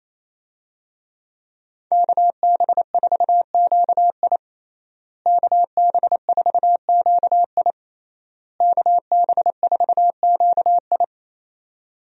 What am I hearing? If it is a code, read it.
Morse code, heard as a keyed tone, about 27 wpm, 700 Hz. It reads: KB4QI KB4QI KB4QI
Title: USA Call Signs - 5 Characters - Repeated 3x - 28wpm